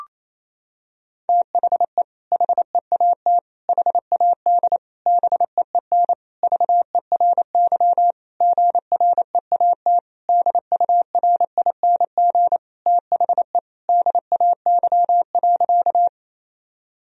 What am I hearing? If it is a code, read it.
Morse code, heard as a keyed tone, about 28 wpm, 700 Hz. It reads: THE HEAT HAD BEEN VERY GREAT DURING THE DAY.
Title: Sentences from Top 1000 Words - Review Part 3 - 28wpm